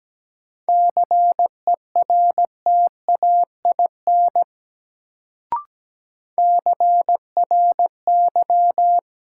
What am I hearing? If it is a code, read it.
Morse code, heard as a keyed tone, about 17 wpm, 700 Hz. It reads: CERTAIN CRY
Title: Sentences from Top 500 Words - New Word Drill - 17wpm